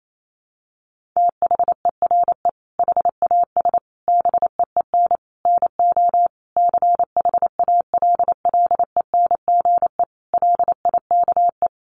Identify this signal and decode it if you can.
Morse code, heard as a keyed tone, about 28 wpm, 700 Hz. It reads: THERE HAS BEEN NO CHALLENGE LIKE
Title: Sentences from Top 1500 Words 28wpm